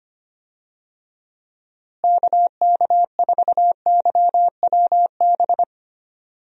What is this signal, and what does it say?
Morse code, heard as a keyed tone, about 25 wpm, 700 Hz. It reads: KK4YWB
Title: USA Call Signs - 6 Characters - Repeated 3x - 25wpm